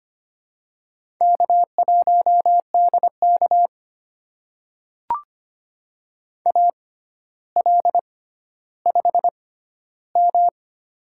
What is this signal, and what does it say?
Morse code, heard as a keyed tone, about 25 wpm, spaced as for 13 wpm, 700 Hz. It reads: K1DK AL5M
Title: USA Call Signs - 4 Characters - 6x Character Spacing - 25wpm